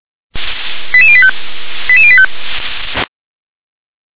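Five-tone selective call (ZVEI), heard as quick runs of five tones, tones of about 70 ms. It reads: 88085 88085
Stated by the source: ZVEI five-tone selective call 88702 received by SDR.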